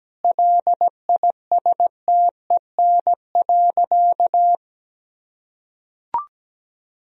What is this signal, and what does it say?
Morse code, heard as a keyed tone, about 17 wpm, 700 Hz. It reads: LISTEN.
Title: Sentences from Top 1000 Words - Review Part 3 - 17wpm